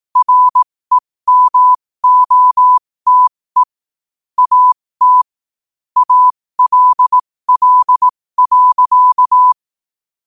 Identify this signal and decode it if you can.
Morse code, heard as a keyed tone, about 18 wpm, spaced as for 15 wpm, 1000 Hz. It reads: REMOTE AT ALL.
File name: qotdfiles / 15wpm